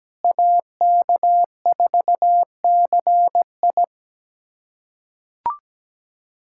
Morse code, heard as a keyed tone, about 17 wpm, 700 Hz. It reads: AK4CI
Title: USA Callsign DE Callsign - 5 Characters - 17wpm